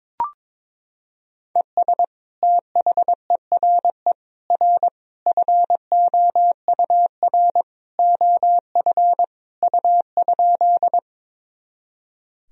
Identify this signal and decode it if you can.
Morse code, heard as a keyed tone, about 21 wpm, 700 Hz. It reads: ES THERE R FOUR OF U?
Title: Sentences from Top 300 Words and Abbreviations 22wpm